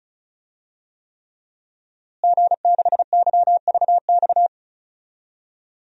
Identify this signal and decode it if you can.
Morse code, heard as a keyed tone, about 36 wpm, 700 Hz. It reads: G6YVX
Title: DXCC-100 Call Signs - 5 Characters - Repeated 3x - 35wpm